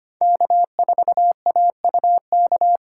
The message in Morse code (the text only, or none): K4AUK